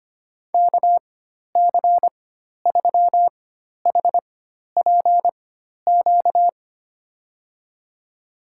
Morse code, heard as a keyed tone, about 25 wpm, spaced as for 12 wpm, 700 Hz. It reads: KC3HPQ
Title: USA Call Signs - 4x Character Spacing - 25wpm